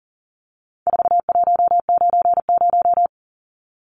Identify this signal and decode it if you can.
Morse code, heard as a keyed tone, about 40 wpm, 700 Hz. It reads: 4190
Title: Numbers - 4-Digit 40wpm